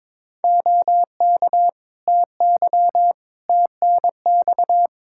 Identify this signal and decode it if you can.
Morse code, heard as a keyed tone, about 22 wpm, 700 Hz. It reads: OK TY TNX
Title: K1USN SST QSO Practice 22wpm